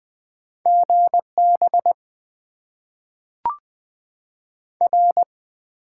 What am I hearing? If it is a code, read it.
Morse code, heard as a keyed tone, about 20 wpm, 700 Hz. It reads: GB R
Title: Top 90 QSO Elements - 8x Character Spacing - 20wpm